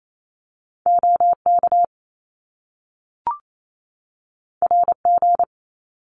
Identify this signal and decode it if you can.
Morse code, heard as a keyed tone, about 28 wpm, 700 Hz. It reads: OK RG
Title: Top Two Letter Combinations in Words 28wpm